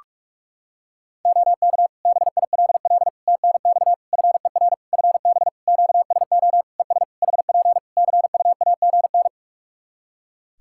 Morse code, heard as a keyed tone, about 44 wpm, 700 Hz. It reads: OK BILL TNX FER FB QSO ES HP CUAGN
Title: Practice QSO - Exchange 4 - Ending - 45wpm